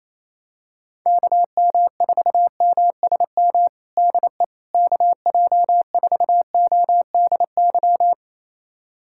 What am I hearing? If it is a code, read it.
Morse code, heard as a keyed tone, about 28 wpm, 700 Hz. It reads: KM4MSM DE KJ4ODY
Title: USA Callsign DE Callsign 28wpm